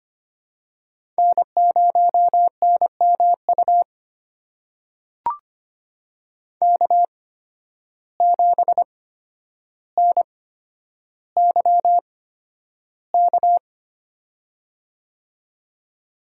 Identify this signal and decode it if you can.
Morse code, heard as a keyed tone, about 25 wpm, spaced as for 11 wpm, 700 Hz. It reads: N0NMU K7NYK
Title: USA Call Signs - 5 Characters - 8x Character Spacing - 25wpm